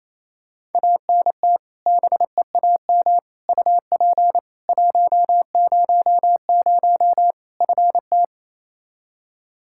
Morse code, heard as a keyed tone, about 28 wpm, 700 Hz. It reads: ANT BEAM UP 100 FT
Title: K1USN SST QSO Practice 28wpm